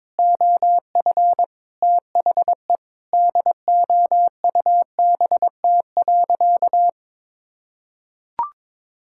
Morse code, heard as a keyed tone, about 22 wpm, 700 Hz. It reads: OF THE DOUBT.
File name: Sentences with 7-Word Phrases - 22wpm